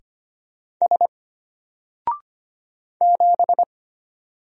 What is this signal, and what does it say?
Morse code, heard as a keyed tone, about 25 wpm, 700 Hz. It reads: S 7
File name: Single-Letter-Number-25wpm